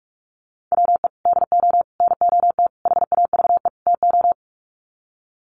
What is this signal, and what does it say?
Morse code, heard as a keyed tone, about 44 wpm, 700 Hz. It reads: WE DO NOT HAVE TO
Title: Sets of 5 Words 45wpm